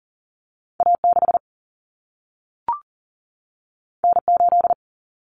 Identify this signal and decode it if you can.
Morse code, heard as a keyed tone, about 40 wpm, 700 Hz. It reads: A6 N8
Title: Common Cut Numbers - 2-Digit 40wpm